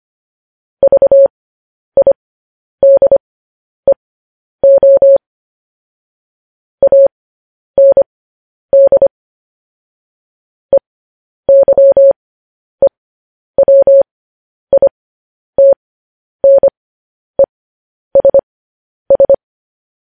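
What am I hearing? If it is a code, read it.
Morse code, heard as a keyed tone, about 25 wpm, spaced as for 10 wpm, 550 Hz. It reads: VIDEO AND EYEWITNESS
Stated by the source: Morse code transcription of news headlines sent at 10 wpm.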